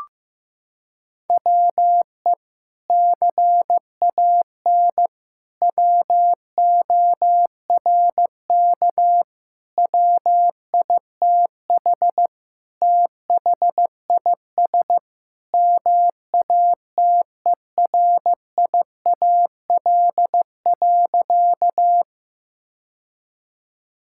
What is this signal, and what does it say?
Morse code, heard as a keyed tone, about 16 wpm, 700 Hz. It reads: WE CAN WORK WITH THIS MATERIAL.
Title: Sentences from Top 1000 Words - Review Part 5 - 15wpm